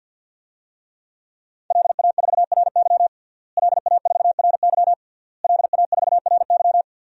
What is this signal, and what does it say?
Morse code, heard as a keyed tone, about 49 wpm, 700 Hz. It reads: LA4RY LA4RY LA4RY
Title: DXCC-100 Call Signs - 5 Characters - Repeated 3x - 50wpm